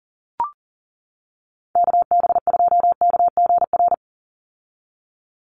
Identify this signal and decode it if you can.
Morse code, heard as a keyed tone, about 38 wpm, 700 Hz. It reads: KB2KGR
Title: USA Call Signs 40wpm